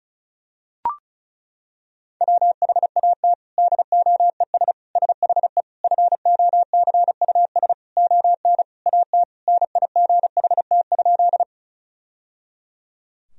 Morse code, heard as a keyed tone, about 36 wpm, 700 Hz. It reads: WHAT DOES SHE FOCUS ON AT NIGHT?
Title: Sentences from Top 1000 Words - Review Part 4 - 35wpm